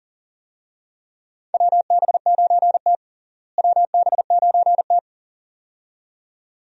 Morse code, heard as a keyed tone, about 40 wpm, 700 Hz. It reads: WB9T WB9T
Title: USA Call Signs - 4 Characters - Repeated 2x - 40wpm